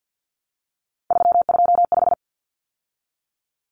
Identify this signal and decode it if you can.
Morse code, heard as a keyed tone, about 50 wpm, 700 Hz. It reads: 325
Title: Common Cut Numbers - 3-Digit 50wpm